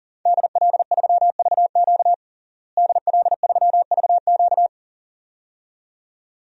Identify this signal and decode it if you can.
Morse code, heard as a keyed tone, about 37 wpm, 700 Hz. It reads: DL3VQ DL3VQ
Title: DXCC-100 Call Signs - 5 Characters - Repeated 2x - 40wpm